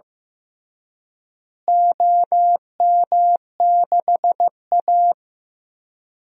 Morse code, heard as a keyed tone, about 15 wpm, 700 Hz. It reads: OM6A
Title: DXCC-100 Call Signs - 4 Characters - Repeated 3x - 15wpm